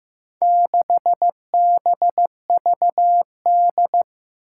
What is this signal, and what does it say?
Morse code, heard as a keyed tone, about 15 wpm, 700 Hz. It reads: 6BVD
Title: Four Challenging Letter-Number Combinations 15wpm